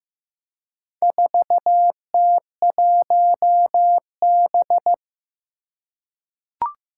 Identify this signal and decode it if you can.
Morse code, heard as a keyed tone, about 15 wpm, 700 Hz. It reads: 4T1B
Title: Four Challenging Letter-Number Combinations 15wpm